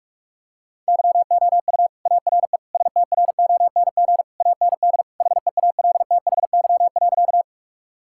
Morse code, heard as a keyed tone, about 46 wpm, 700 Hz. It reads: YOU ARE STRONG AND HEALTHY.
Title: Sentences from Top 1000 Words - Review Part 5 - 45wpm